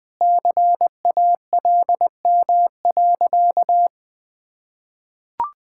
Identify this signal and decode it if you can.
Morse code, heard as a keyed tone, about 20 wpm, 700 Hz. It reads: CALM.
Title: Sentences with 5-Word Phrases - 20wpm